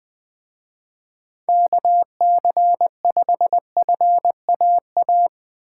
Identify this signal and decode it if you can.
Morse code, heard as a keyed tone, about 23 wpm, 700 Hz. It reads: KC5FAA
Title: Speed Racing - USA 2x3 Call Signs 20wpm